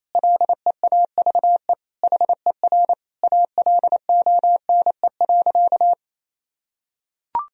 Morse code, heard as a keyed tone, about 28 wpm, 700 Hz. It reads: LEAVE HER ALONE.
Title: Sentences from Top 1000 Words - Review Part 3 - 28wpm